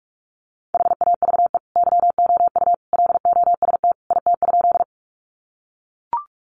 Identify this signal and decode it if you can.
Morse code, heard as a keyed tone, about 44 wpm, 700 Hz. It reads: HAVE YOU LOST IT?
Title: Sentences from Top 1400 Words 45wpm